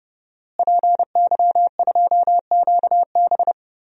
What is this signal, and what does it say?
Morse code, heard as a keyed tone, about 30 wpm, 700 Hz. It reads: PY2QB
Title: DXCC-100 Call Signs - 5 Characters - Repeated 2x - 30wpm